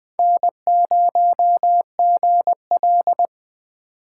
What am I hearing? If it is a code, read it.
Morse code, heard as a keyed tone, about 24 wpm, 700 Hz. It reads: N0GL
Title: Speed Racing - USA 1x2 Call Signs 20wpm